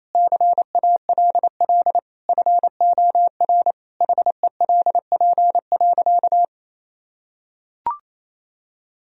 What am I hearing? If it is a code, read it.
Morse code, heard as a keyed tone, about 27 wpm, 700 Hz. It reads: CALL FOR HELP.
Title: Sentences from Top 1000 Words - Review Part 4 - 28wpm